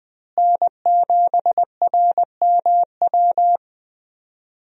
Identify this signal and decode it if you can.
Morse code, heard as a keyed tone, about 20 wpm, 700 Hz. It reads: N7RMW
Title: USA Call Signs - 5 Characters - Repeated 3x - 20wpm